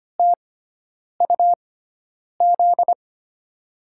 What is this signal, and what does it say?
Morse code, heard as a keyed tone, about 25 wpm, spaced as for 9 wpm, 700 Hz. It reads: TUZ
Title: USA Call Signs - 5 Characters - 6x Character Spacing - 25wpm